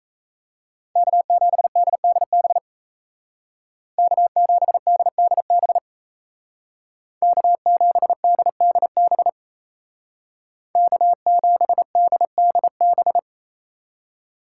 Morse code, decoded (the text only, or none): K7DDB K7DDB K7DDB K7DDB